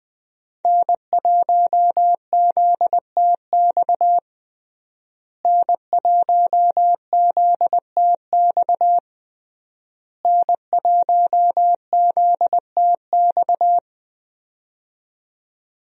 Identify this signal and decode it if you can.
Morse code, heard as a keyed tone, about 20 wpm, 700 Hz. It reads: N1ZTX N1ZTX N1ZTX
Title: USA Call Signs - 5 Characters - Repeated 3x - 20wpm